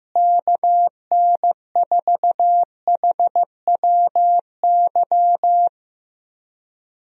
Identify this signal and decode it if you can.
Morse code, heard as a keyed tone, about 15 wpm, 700 Hz. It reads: KN4HWY
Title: USA Call Signs - 6 Characters - Repeated 3x - 15wpm